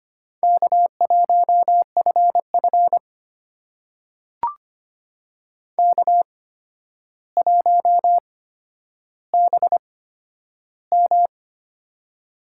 Morse code, heard as a keyed tone, about 25 wpm, spaced as for 11 wpm, 700 Hz. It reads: K1FF K1BM
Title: USA Call Signs - 4 Characters - 8x Character Spacing - 25wpm